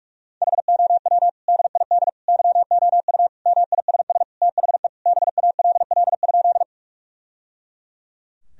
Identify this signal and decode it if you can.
Morse code, heard as a keyed tone, about 41 wpm, 700 Hz. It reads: HOW DID YOU MISS THE BALL?
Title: Sentences from Top 1000 Words - Review Part 3 - 45wpm